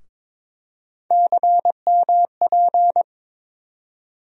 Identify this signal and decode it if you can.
Morse code, heard as a keyed tone, about 22 wpm, 700 Hz. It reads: CMP
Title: USA Call Sign Postfixes - 3 Letters 22wpm